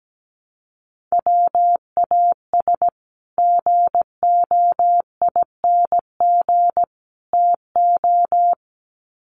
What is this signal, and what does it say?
Morse code, heard as a keyed tone, about 17 wpm, 700 Hz. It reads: WAS GOING TO
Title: Sets of 3 Words 17wpm